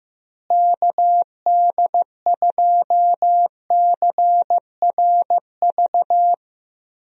Morse code, heard as a keyed tone, about 15 wpm, 700 Hz. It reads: KD2CRV
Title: USA Call Signs - 6 Characters - Repeated 2x - 15wpm